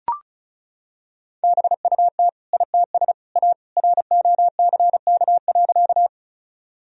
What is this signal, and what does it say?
Morse code, heard as a keyed tone, about 34 wpm, 700 Hz. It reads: BUT ITS A ROCK.
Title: Sentences from Top 1000 Words - Review Part 2 - 35wpm